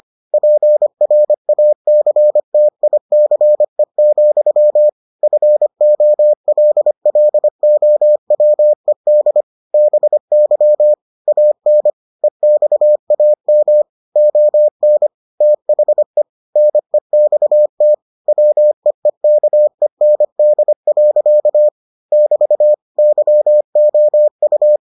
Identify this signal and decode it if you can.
Morse code, heard as a keyed tone, about 25 wpm, 600 Hz. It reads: PRACTICE, FOLLOWED BY AN EXAM ON THE NEXT WEEKEND. <BT> YOU